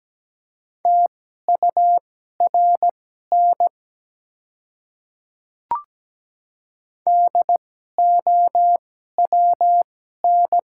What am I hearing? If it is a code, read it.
Morse code, heard as a keyed tone, about 17 wpm, spaced as for 12 wpm, 700 Hz. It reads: TURN DOWN
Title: 2x Character Spacing - Top 300 Words - 17wpm